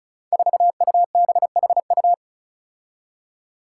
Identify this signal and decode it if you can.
Morse code, heard as a keyed tone, about 36 wpm, 700 Hz. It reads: 4UBHU
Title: Five Challenging Letter-Number Combinations 35wpm